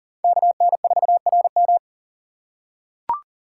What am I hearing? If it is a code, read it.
Morse code, heard as a keyed tone, about 39 wpm, 700 Hz. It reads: KN4RM
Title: USA Call Signs - 5 Characters - 3x Character Spacing - 40wpm